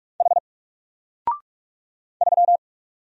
Morse code, heard as a keyed tone, about 44 wpm, 700 Hz. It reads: H 3
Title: Single Character (Letter:Number:Punctuation:Prosign) 45wpm